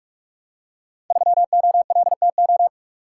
Morse code, heard as a keyed tone, about 44 wpm, 700 Hz. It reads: 3OLTO
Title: Code Groups - 5 Letter:Numbers - 45wpm